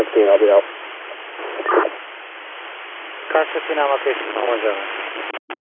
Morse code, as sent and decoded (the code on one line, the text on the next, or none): none